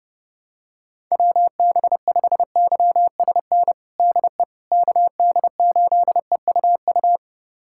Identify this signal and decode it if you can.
Morse code, heard as a keyed tone, about 30 wpm, 700 Hz. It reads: WB5YSN DE KD8EUU
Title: USA Callsign DE Callsign - 6 Characters - 30wpm